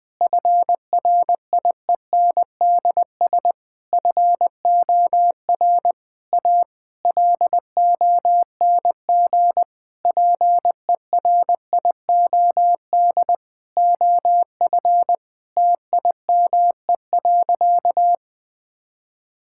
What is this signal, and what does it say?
Morse code, heard as a keyed tone, about 20 wpm, 700 Hz. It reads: FRIENDS FOR A LONG PERIOD OF TIME.
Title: Sentences with 6-Word Phrases - 20wpm